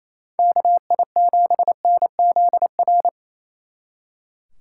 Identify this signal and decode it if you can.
Morse code, heard as a keyed tone, about 34 wpm, 700 Hz. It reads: KI7NZR
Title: Speed Racing - USA 2x3 Call Signs 28wpm